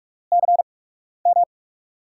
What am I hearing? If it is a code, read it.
Morse code, heard as a keyed tone, about 44 wpm, 700 Hz. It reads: C M
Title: USA Call Signs - 8x Character Spacing - 45wpm